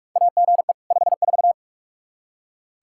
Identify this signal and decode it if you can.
Morse code, heard as a keyed tone, about 41 wpm, 700 Hz. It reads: AGE 54
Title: Practice QSO - Exchange 3 - More Info - 45wpm